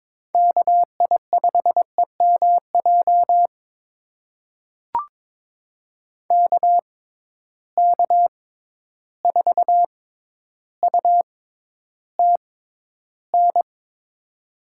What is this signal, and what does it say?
Morse code, heard as a keyed tone, about 21 wpm, spaced as for 11 wpm, 700 Hz. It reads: KI5EMJ KK4UTN
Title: USA Call Signs - 6 Characters - 6x Character Spacing - 22wpm